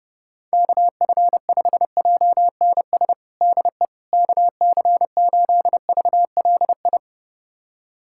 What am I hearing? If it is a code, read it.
Morse code, heard as a keyed tone, about 30 wpm, 700 Hz. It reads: KF5JNS DE KC8VLI
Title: USA Callsign DE Callsign - 6 Characters - 30wpm